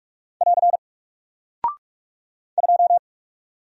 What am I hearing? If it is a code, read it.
Morse code, heard as a keyed tone, about 44 wpm, 700 Hz. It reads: <AR> 2
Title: Single Character (Letter:Number:Punctuation:Prosign) Alt 45wpm